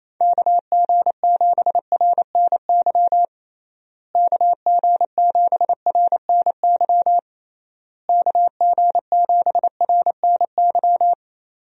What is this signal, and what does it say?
Morse code, heard as a keyed tone, about 27 wpm, 700 Hz. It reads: KG7RNY KG7RNY KG7RNY
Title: USA Call Signs - 6 Characters - Repeated 3x - 28wpm